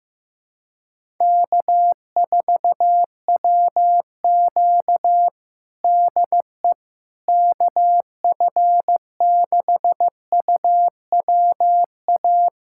Morse code, heard as a keyed tone, about 15 wpm, 700 Hz. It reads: K4WQ DE KF6UWA
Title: USA Callsign DE Callsign 15wpm